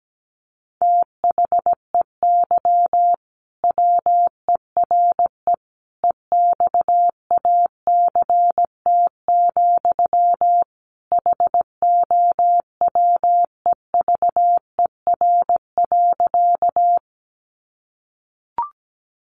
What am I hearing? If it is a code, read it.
Morse code, heard as a keyed tone, about 17 wpm, 700 Hz. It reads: THEY WERE EXACT, HOWEVER.
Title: Sentences from Top 1300 Words 17wpm